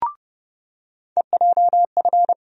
Morse code, heard as a keyed tone, about 30 wpm, 700 Hz. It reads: EJF
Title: USA Call Sign Postfixes - 3 Letters 30wpm